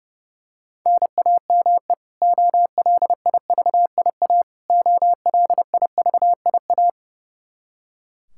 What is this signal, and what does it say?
Morse code, heard as a keyed tone, about 31 wpm, 700 Hz. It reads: NAME OLIVIA OLIVIA
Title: Practice QSO - Exchange 1 - Beginning - 30wpm